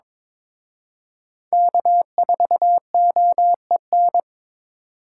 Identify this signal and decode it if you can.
Morse code, heard as a keyed tone, about 26 wpm, 700 Hz. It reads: K4OEN
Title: Speed Racing - USA 1x3 Call Signs 22wpm